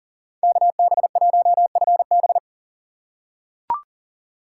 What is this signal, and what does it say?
Morse code, heard as a keyed tone, about 40 wpm, 700 Hz. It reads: KB1FB
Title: USA Call Signs - 5 Characters - 3x Character Spacing - 40wpm